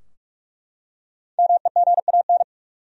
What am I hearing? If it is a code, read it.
Morse code, heard as a keyed tone, about 47 wpm, 700 Hz. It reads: MEGAN
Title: USA Common Names - 45wpm